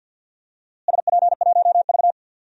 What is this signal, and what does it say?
Morse code, heard as a keyed tone, about 47 wpm, 700 Hz. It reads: SP1V
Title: DXCC-100 Call Signs - 4 Characters - Repeated 2x - 50wpm